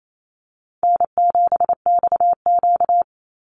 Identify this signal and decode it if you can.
Morse code, heard as a keyed tone, about 28 wpm, 700 Hz. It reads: N7XQ
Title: USA Call Signs 28wpm